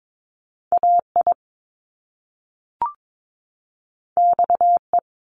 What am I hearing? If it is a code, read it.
Morse code, heard as a keyed tone, about 22 wpm, 700 Hz. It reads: AI XE
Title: Top Two Letter Combinations in Words 22wpm